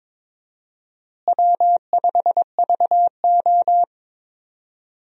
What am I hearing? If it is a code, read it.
Morse code, heard as a keyed tone, about 22 wpm, 700 Hz. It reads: W5VO
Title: USA Call Signs - 4 Characters - Repeated 2x - 22wpm